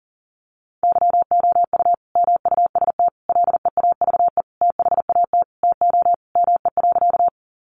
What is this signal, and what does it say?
Morse code, heard as a keyed tone, about 40 wpm, 700 Hz. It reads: YOU MUST LEAVE THAT TO ME.
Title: Sentences from Top 400 Words 40wpm